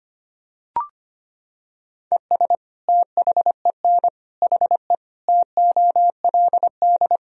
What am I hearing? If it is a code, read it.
Morse code, heard as a keyed tone, about 25 wpm, 700 Hz. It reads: ES THEN HE TOLD
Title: Sentences from Top 400 Words and Abbreviations 25wpm